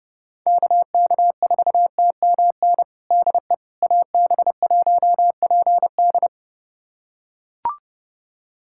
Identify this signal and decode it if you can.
Morse code, heard as a keyed tone, about 30 wpm, 700 Hz. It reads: KK4TMN DE AB1PD
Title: USA Callsign DE Callsign 30wpm